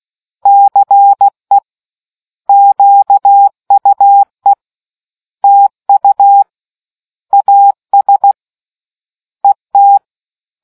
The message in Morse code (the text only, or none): CE QUE TU AS ET